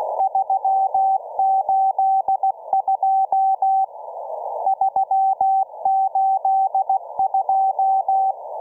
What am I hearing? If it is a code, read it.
Morse code, heard as a keyed tone, about 16 wpm, 770 Hz. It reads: 382 382